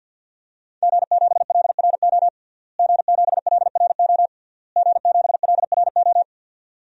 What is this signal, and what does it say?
Morse code, heard as a keyed tone, about 51 wpm, 700 Hz. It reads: G7LRO G7LRO G7LRO
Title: DXCC-100 Call Signs - 5 Characters - Repeated 3x - 50wpm